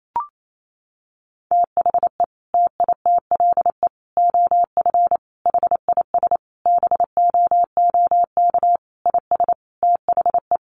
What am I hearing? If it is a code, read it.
Morse code, heard as a keyed tone, about 28 wpm, 700 Hz. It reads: THE TITLE OF HIS BOOK IS THE
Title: Sentences from Top 1200 Words 28wpm